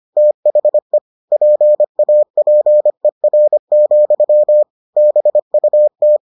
Morse code, heard as a keyed tone, about 25 wpm, 600 Hz. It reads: THE PAPER, BUT